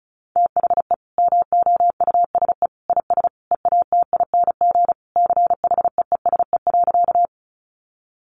Morse code, heard as a keyed tone, about 34 wpm, 700 Hz. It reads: THE MOUSE IS EATING CHEESE.
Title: Easy Phrases 35wpm